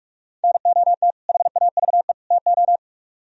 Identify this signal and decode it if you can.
Morse code, heard as a keyed tone, about 62 wpm, spaced as for 53 wpm, 700 Hz. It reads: NOT HAVE TO
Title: Speed-Racing - Sets of 3 Words 45wpm